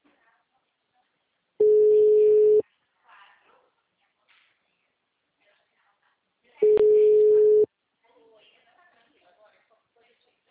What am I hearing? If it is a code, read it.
Ringback tone (Europe, 425 Hz), with speech.
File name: out- -2009- - - .88193